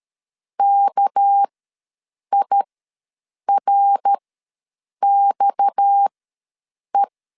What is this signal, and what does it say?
Morse code, heard as a keyed tone, about 13 wpm, spaced as for 7 wpm, 800 Hz. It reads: KIRXE
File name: LEZIONI / 8 YJ45 / 8 Y ws 12wpm 3409